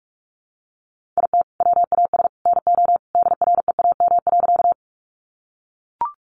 Morse code, heard as a keyed tone, about 44 wpm, 700 Hz. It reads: IT WAS NO DREAM.
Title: Sentences from Top 1500 Words 45wpm